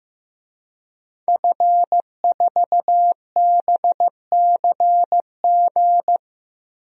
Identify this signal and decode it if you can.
Morse code, heard as a keyed tone, about 15 wpm, 700 Hz. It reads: F4BCG
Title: DXCC-100 Call Signs - 5 Characters - 15wpm